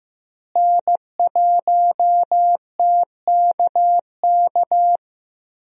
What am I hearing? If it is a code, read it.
Morse code, heard as a keyed tone, about 15 wpm, 700 Hz. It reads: N1TKK
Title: USA Call Signs - 5 Characters - Repeated 3x - 15wpm